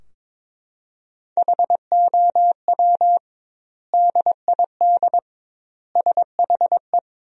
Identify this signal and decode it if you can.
Morse code, heard as a keyed tone, about 22 wpm, 700 Hz. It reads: HOW DID SHE
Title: 2x Word-Spacing - Sentences from Top 100 Words - 22wpm